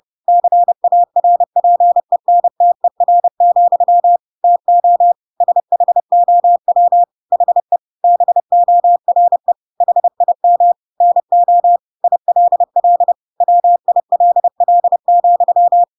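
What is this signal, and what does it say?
Morse code, heard as a keyed tone, about 30 wpm, 700 Hz. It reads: CARPENTER, TO SHOW HE BORE HIM NO ILL WILL,
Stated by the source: Chapter 2 of The Adventures of Pinocchio read as a book in Morse code (CW) at 30 wpm, by ik1hge.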